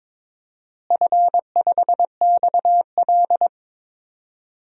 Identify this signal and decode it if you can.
Morse code, heard as a keyed tone, about 22 wpm, 700 Hz. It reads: F5XL
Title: DXCC-100 Call Signs - 4 Characters - Repeated 3x - 22wpm